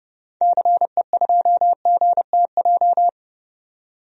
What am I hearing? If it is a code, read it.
Morse code, heard as a keyed tone, about 31 wpm, 700 Hz. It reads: CE2GTJ
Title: DXCC-100 Call Signs - 6 Characters - 30wpm